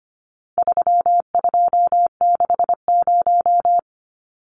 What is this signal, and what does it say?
Morse code, heard as a keyed tone, about 25 wpm, 700 Hz. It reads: 3260
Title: Numbers - 4-Digit 25wpm